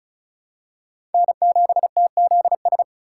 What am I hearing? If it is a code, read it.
Morse code, heard as a keyed tone, about 36 wpm, 700 Hz. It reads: N7TZS
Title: USA Call Signs - 5 Characters - 4x Character Spacing - 35wpm